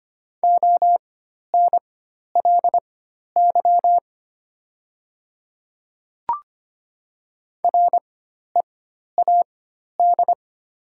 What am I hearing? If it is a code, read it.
Morse code, heard as a keyed tone, about 25 wpm, spaced as for 12 wpm, 700 Hz. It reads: ONLY READ
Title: 4x Character Spacing - Top 300 Words - 25wpm